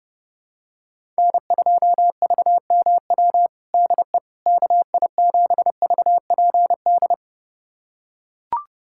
Morse code, heard as a keyed tone, about 32 wpm, 700 Hz. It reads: N2VMW DE KI7VPD
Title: USA Callsign DE Callsign 30wpm